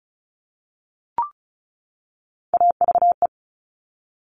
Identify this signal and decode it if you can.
Morse code, heard as a keyed tone, about 36 wpm, 700 Hz. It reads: AVE
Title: Top Three Letter Combinations in Words 35wpm